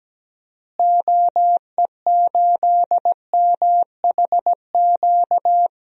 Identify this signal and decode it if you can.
Morse code, heard as a keyed tone, about 17 wpm, 700 Hz. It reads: OE8MHQ
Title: DXCC-100 Call Signs - 6 Characters - Repeated 2x - 17wpm